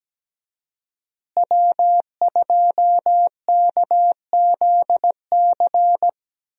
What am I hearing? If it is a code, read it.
Morse code, heard as a keyed tone, about 17 wpm, 700 Hz. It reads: W2KZC
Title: USA Call Signs - 5 Characters - Repeated 3x - 17wpm